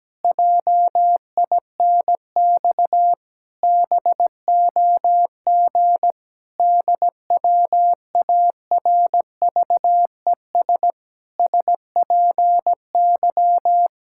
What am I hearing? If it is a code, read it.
Morse code, heard as a keyed tone, about 17 wpm, 700 Hz. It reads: JINX BOG DWARVES SPY
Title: Pangram Challenge v2 - 17wpm